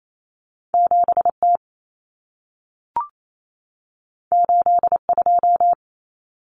Morse code, heard as a keyed tone, about 28 wpm, 700 Hz. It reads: 7T 82
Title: Common Cut Numbers - 2-Digit 28wpm